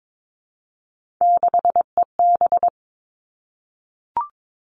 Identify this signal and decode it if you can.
Morse code, heard as a keyed tone, about 22 wpm, 700 Hz. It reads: 6EB
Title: Three Challenging Letter-Number Combinations 22wpm